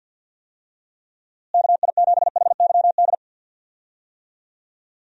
Morse code, heard as a keyed tone, about 47 wpm, 700 Hz. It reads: KI7HYD
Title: USA Call Signs - 6 Characters - Repeated 3x - 50wpm